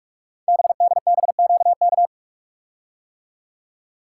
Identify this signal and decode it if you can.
Morse code, heard as a keyed tone, about 44 wpm, 690 Hz. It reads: BDBQK
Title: Code Groups - 5 Letters - 45wpm